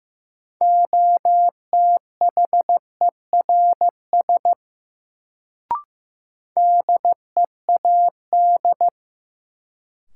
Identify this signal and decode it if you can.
Morse code, heard as a keyed tone, about 15 wpm, 700 Hz. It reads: OTHERS DEAD
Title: Sentences from Top 1500 Words - New Word Drill Alt - 15wpm